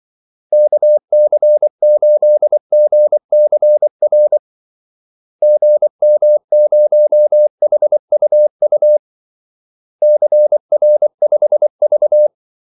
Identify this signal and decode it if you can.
Morse code, heard as a keyed tone, about 24 wpm, 600 Hz. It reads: KC8GCR GM0HUU CR5V
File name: Callsigns-24wpm